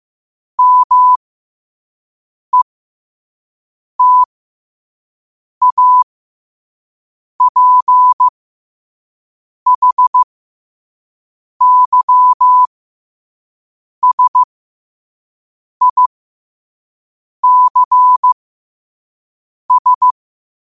Morse code, heard as a keyed tone, about 15 wpm, spaced as for 5 wpm, 1000 Hz. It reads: METAPHYSICS
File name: qotdfiles / 5wpm